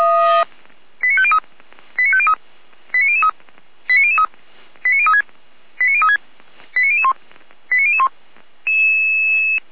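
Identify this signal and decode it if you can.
Five-tone selective call (ZVEI), heard as quick runs of five tones, tones of about 70 ms. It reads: 89503 89503 89003 89003 89036 89036 89001 89001
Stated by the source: ZVEI five-tone selective call received with an SDR, from the Feuerwehr Orlamünde (fire brigade).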